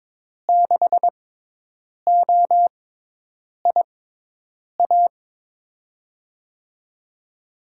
Morse code, heard as a keyed tone, about 22 wpm, spaced as for 8 wpm, 700 Hz. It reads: 6OIA
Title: USA Call Signs - 6x Character Spacing - 22wpm